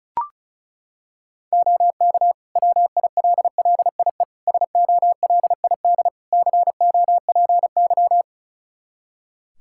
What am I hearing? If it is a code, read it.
Morse code, heard as a keyed tone, about 36 wpm, 700 Hz. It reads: OK WILLIE SOLID COPY
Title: Practice QSO - Exchange 3 - More Info - 35wpm